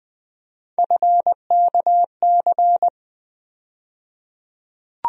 Morse code, heard as a keyed tone, about 20 wpm, 700 Hz. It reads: FKC
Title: USA Call Sign Postfixes - 3 Letters 20wpm